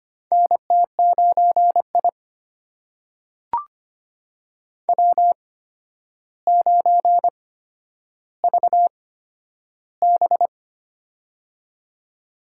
Morse code, heard as a keyed tone, about 25 wpm, spaced as for 11 wpm, 700 Hz. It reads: NT9I W9VB
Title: USA Call Signs - 4 Characters - 8x Character Spacing - 25wpm